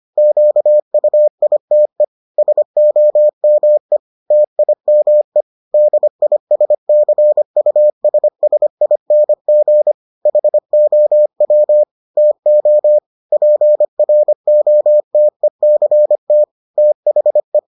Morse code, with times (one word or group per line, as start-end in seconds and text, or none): 0.2-2.0 QUITE
2.4-4.0 SOME
4.3-5.4 TIME
5.7-9.9 DISCUSSING
10.3-11.8 HOW
12.2-13.0 TO
13.3-16.4 PROTECT
16.8-17.6 THE